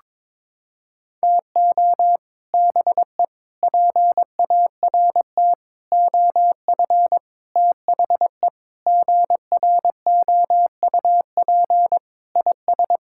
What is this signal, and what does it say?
Morse code, heard as a keyed tone, about 22 wpm, 700 Hz. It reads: TO BE PART OF THE GROUP IS
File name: practice / 22wpm / Sentences with 4-Word Phrases - 22wpm